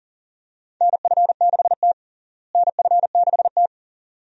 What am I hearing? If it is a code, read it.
Morse code, heard as a keyed tone, about 37 wpm, 700 Hz. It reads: NF6T NF6T
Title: USA Call Signs - 4 Characters - Repeated 2x - 40wpm